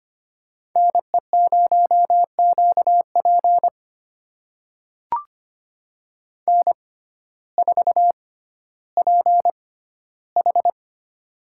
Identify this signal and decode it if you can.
Morse code, heard as a keyed tone, about 25 wpm, spaced as for 14 wpm, 700 Hz. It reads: NE0QP N4PH
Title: USA Call Signs - 6x Character Spacing - 25wpm